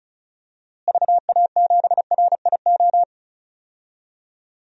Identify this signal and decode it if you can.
Morse code, heard as a keyed tone, about 36 wpm, 700 Hz. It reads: VA7RIO
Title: DXCC-100 Call Signs - 6 Characters - Repeated 2x - 35wpm